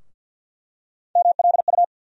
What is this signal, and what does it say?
Morse code, heard as a keyed tone, about 51 wpm, 700 Hz. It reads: MLU
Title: USA Call Sign Postfixes - 3 Letters 50wpm